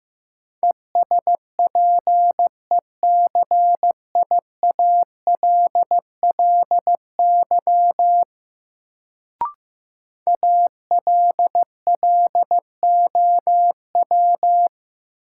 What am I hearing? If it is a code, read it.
Morse code, heard as a keyed tone, about 15 wpm, 700 Hz. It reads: ESPECIALLY ALLOW
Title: Sentences from Top 700 Words - New Word Drill - 15wpm